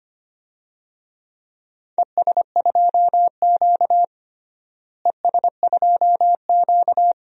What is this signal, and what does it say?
Morse code, heard as a keyed tone, about 25 wpm, 700 Hz. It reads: ES2Q ES2Q
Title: DXCC-100 Call Signs - 4 Characters - Repeated 2x - 25wpm